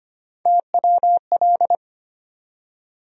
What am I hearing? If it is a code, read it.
Morse code, heard as a keyed tone, about 25 wpm, 700 Hz. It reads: TWL
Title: USA Call Sign Postfixes - 3 Letters 25wpm